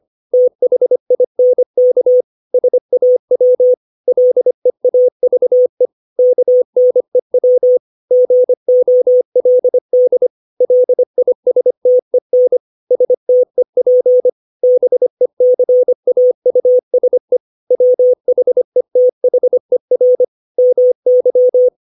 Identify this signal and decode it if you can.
Morse code, heard as a keyed tone, about 25 wpm, 500 Hz. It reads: THINK SAW LEAVE KNEW GOLD LISTEN STEP BECAUSE WHETHER MY
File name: 632 words / 632 Words 25WPM